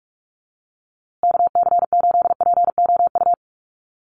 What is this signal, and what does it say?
Morse code, heard as a keyed tone, about 44 wpm, 700 Hz. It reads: KC8POU
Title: USA Call Signs Repeated 3x 45wpm